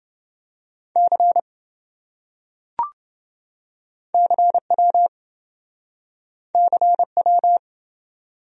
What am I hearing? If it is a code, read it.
Morse code, heard as a keyed tone, about 27 wpm, 700 Hz. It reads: C CW CW